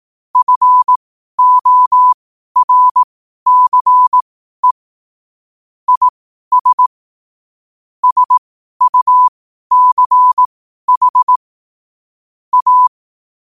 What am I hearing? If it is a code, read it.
Morse code, heard as a keyed tone, about 18 wpm, spaced as for 12 wpm, 1000 Hz. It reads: FORCE IS SUCH A
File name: qotdfiles / 12wpm